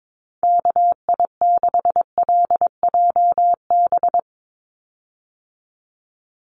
Morse code, heard as a keyed tone, about 22 wpm, 700 Hz. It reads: KI6LJB
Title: USA Call Signs 22wpm